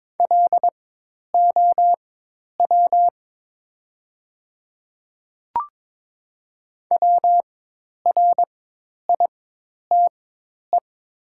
Morse code, heard as a keyed tone, about 22 wpm, spaced as for 10 wpm, 700 Hz. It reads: LOW WRITE
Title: 4x Character Spacing - Top 200 Words - 22wpm